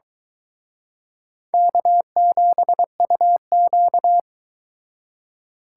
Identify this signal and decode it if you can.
Morse code, heard as a keyed tone, about 24 wpm, 700 Hz. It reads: K7UQ